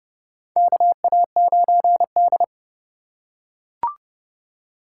Morse code, heard as a keyed tone, about 30 wpm, 700 Hz. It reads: KA9D
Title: USA Call Signs - 4 Characters - 4x Character Spacing - 30wpm